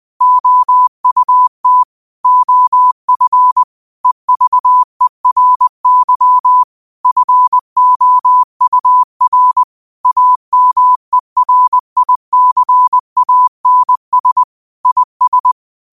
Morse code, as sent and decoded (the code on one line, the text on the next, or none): --- ..- - / --- ..-. / . ...- . .-. -.-- / ..-. --- ..- .-. / .- -- . .-. .. -.-. .- -. ... / .. ...
OUT OF EVERY FOUR AMERICANS IS